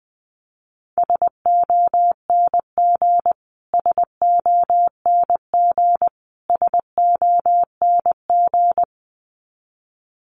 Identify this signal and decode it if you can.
Morse code, heard as a keyed tone, about 20 wpm, 700 Hz. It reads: SONG SONG SONG
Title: Top 500 Words Repeated 20wpm